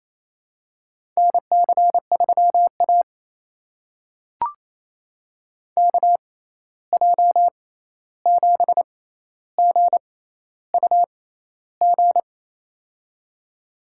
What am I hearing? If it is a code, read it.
Morse code, heard as a keyed tone, about 28 wpm, spaced as for 13 wpm, 700 Hz. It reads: NC3A KJ7GUG
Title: USA Call Signs - 6x Character Spacing - 28wpm